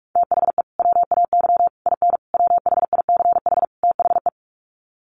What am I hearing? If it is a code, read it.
Morse code, heard as a keyed tone, about 44 wpm, 700 Hz. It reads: THE WAY IN WHICH THE
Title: Sets of 5 Words Encore2 45wpm